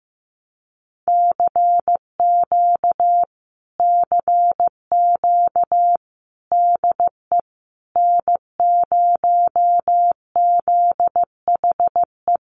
Morse code, heard as a keyed tone, about 15 wpm, 700 Hz. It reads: CQ CQ DE N0ZHE
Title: Common QSO Phrases Encore 15wpm